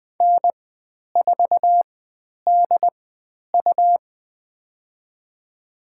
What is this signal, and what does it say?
Morse code, heard as a keyed tone, about 20 wpm, spaced as for 10 wpm, 700 Hz. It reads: N4DU
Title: 109 Lesson 18g - Letter m - Call Signs 20wpm